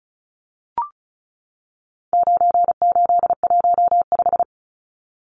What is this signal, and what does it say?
Morse code, heard as a keyed tone, about 34 wpm, 700 Hz. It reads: 9815
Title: Numbers - 4-Digit 35wpm